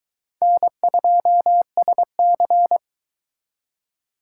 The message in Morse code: N2SC

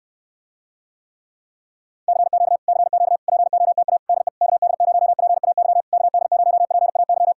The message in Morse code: CQ CQ POTA DE KD0PAY KD0PAY

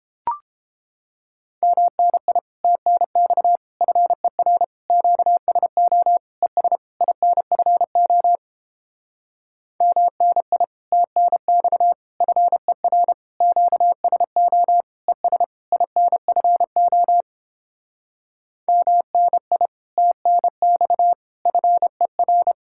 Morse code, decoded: MNI TNX FER QSO ES INFO MNI TNX FER QSO ES INFO MNI TNX FER

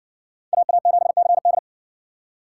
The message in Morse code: AA7CD

